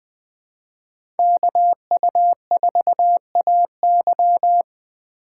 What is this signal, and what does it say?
Morse code, heard as a keyed tone, about 20 wpm, 700 Hz. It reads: KU4AY
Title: Speed Racing - USA 2x2 Call Signs 17wpm